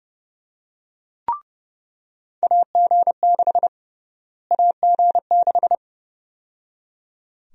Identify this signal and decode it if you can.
Morse code, heard as a keyed tone, about 30 wpm, 700 Hz. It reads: AG6 AG6
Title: USA Call Sign Prefixes - Repeated 2x - 30wpm